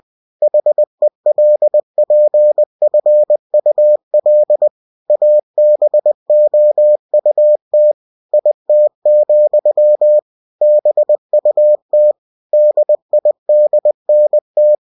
Morse code, heard as a keyed tone, about 20 wpm, 600 Hz. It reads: HELPFUL ABOUT IT, BUT DIDNT